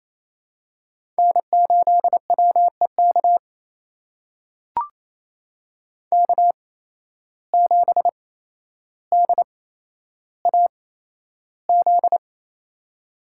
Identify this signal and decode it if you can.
Morse code, heard as a keyed tone, about 28 wpm, spaced as for 12 wpm, 700 Hz. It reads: N8WEK K7DAZ
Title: USA Call Signs - 5 Characters - 8x Character Spacing - 28wpm